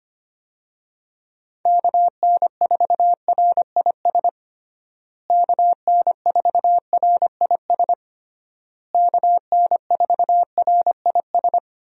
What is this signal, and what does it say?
Morse code, heard as a keyed tone, about 25 wpm, 700 Hz. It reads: KN4RIS KN4RIS KN4RIS
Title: USA Call Signs - 6 Characters - Repeated 3x - 25wpm